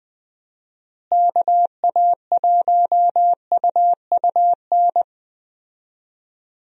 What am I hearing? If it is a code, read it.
Morse code, heard as a keyed tone, about 20 wpm, 700 Hz. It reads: KA1UUN
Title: USA Call Signs - 6 Characters - Repeated 2x - 20wpm